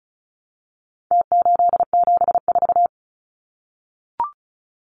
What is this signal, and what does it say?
Morse code, heard as a keyed tone, about 34 wpm, 700 Hz. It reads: T874
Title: Numbers - Contest Style - 4-Digit 35wpm